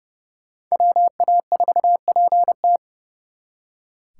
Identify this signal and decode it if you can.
Morse code, heard as a keyed tone, about 38 wpm, 700 Hz. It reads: WA4PT